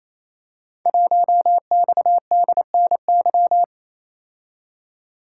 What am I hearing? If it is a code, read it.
Morse code, heard as a keyed tone, about 29 wpm, 700 Hz. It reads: 1XDNY
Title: Code Groups - 5 Letter:Numbers - 28wpm